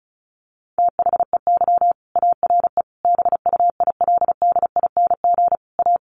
Morse code, heard as a keyed tone, about 34 wpm, 700 Hz. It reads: THEY ARE BUILDING A
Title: Sentences from Top 700 Words 35wpm